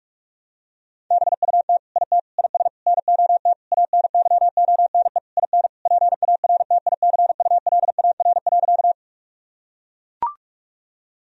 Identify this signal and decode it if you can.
Morse code, heard as a keyed tone, about 41 wpm, 700 Hz. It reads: BUT IT IS NOT ANYONE IN PARTICULAR.
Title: Sentences from Top 1000 Words - Review Part 4 - 45wpm